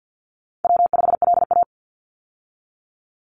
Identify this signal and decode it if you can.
Morse code, heard as a keyed tone, about 50 wpm, 700 Hz. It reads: W5LA
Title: Call Signs - 50wpm